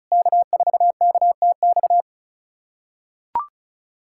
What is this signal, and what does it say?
Morse code, heard as a keyed tone, about 39 wpm, 700 Hz. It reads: K4KTX
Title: SOTA QSO Practice 35wpm